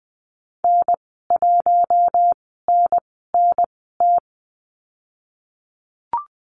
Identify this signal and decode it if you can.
Morse code, heard as a keyed tone, about 20 wpm, spaced as for 15 wpm, 700 Hz. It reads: N1NNT
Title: 055 Lesson 10g - Letter r - Call Signs 20wpm